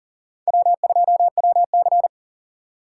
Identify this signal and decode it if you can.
Morse code, heard as a keyed tone, about 40 wpm, 700 Hz. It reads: W2WC
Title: USA Call Signs - 4 Characters - 4x Character Spacing - 40wpm